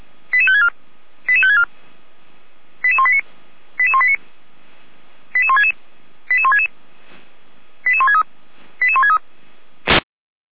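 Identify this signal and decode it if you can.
Five-tone selective call (ZVEI), heard as quick runs of five tones, tones of about 70 ms. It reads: 88564 88564 88189 88189 88166 88166 88163 88163
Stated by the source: ZVEI five-tone selective call received on SDR.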